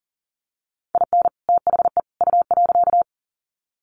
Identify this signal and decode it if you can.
Morse code, heard as a keyed tone, about 40 wpm, 700 Hz. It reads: IN THE U.
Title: Sets of 3 Words 40wpm